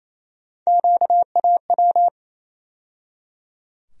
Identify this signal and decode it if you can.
Morse code, heard as a keyed tone, about 28 wpm, 700 Hz. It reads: QAW
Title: USA Call Sign Postfixes - 3 Letters 28wpm